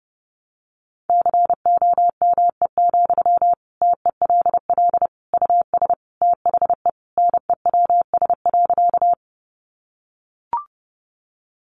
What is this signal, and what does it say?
Morse code, heard as a keyed tone, about 30 wpm, 700 Hz. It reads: COME, TELL US THE NEWS.
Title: Sentences from Top 800 Words 30wpm